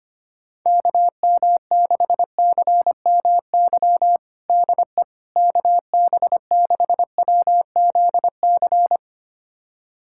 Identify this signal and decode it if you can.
Morse code, heard as a keyed tone, about 25 wpm, 700 Hz. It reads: KM6CMY DE KB6WZC
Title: USA Callsign DE Callsign - 6 Characters - 25wpm